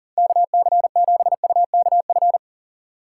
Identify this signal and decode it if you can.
Morse code, heard as a keyed tone, about 40 wpm, 700 Hz. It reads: KC7UKF
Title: USA Call Signs - 2x Character Spacing - 40wpm